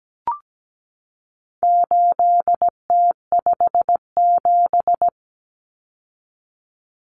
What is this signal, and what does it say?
Morse code, heard as a keyed tone, about 17 wpm, 700 Hz. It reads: 8T57
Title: Common Cut Numbers - 4-Digit 17wpm